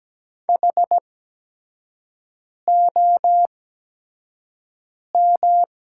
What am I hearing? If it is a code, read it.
Morse code, heard as a keyed tone, about 17 wpm, spaced as for 5 wpm, 700 Hz. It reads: HOM
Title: 8x Character Spacing - Top 200 Words - 17wpm